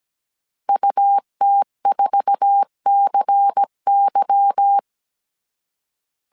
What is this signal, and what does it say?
Morse code, heard as a keyed tone, about 17 wpm, 800 Hz. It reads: UT4CY
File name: CW / DET / 1 9 1500C S C 16wpm 2909